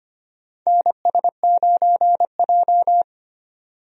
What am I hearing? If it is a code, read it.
Morse code, heard as a keyed tone, about 25 wpm, 700 Hz. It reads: NS9J
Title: USA Call Signs - 4 Characters - 25wpm